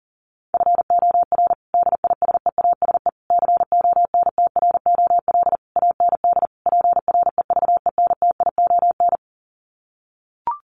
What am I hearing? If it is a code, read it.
Morse code, heard as a keyed tone, about 40 wpm, 700 Hz. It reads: FOR DISEASE CONTROL AND PREVENTION
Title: Sets of 5 Words Encore2 40wpm